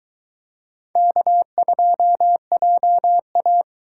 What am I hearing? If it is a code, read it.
Morse code, heard as a keyed tone, about 21 wpm, 700 Hz. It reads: K2JA